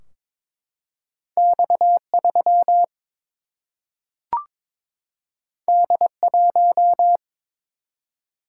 Morse code, heard as a keyed tone, about 22 wpm, 700 Hz. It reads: X 3 D 1
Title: Single-Letter Single-Number 22wpm